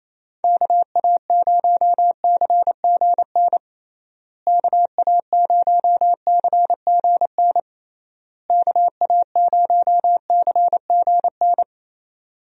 Morse code, heard as a keyed tone, about 28 wpm, 700 Hz. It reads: KA0CGN KA0CGN KA0CGN
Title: USA Call Signs - 6 Characters - Repeated 3x - 28wpm